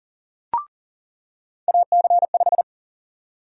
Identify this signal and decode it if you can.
Morse code, heard as a keyed tone, about 40 wpm, 700 Hz. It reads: AC5
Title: USA Call Sign Prefixes - 40wpm